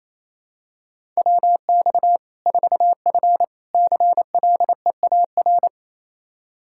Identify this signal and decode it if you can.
Morse code, heard as a keyed tone, about 28 wpm, 700 Hz. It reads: WX 4F CLEAR
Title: Practice QSO - Exchange 2 - Info - 28wpm